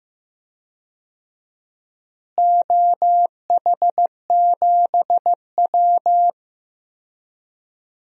Morse code, heard as a keyed tone, about 15 wpm, 700 Hz. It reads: OH7W
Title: DXCC-100 Call Signs - 4 Characters - Repeated 2x - 15wpm